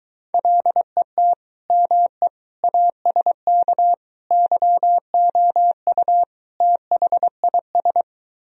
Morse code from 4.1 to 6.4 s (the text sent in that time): YOU